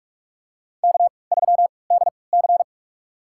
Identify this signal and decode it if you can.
Morse code, heard as a keyed tone, about 46 wpm, spaced as for 25 wpm, 690 Hz. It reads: K3DC
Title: USA Call Signs - 3x Character Spacing - 45wpm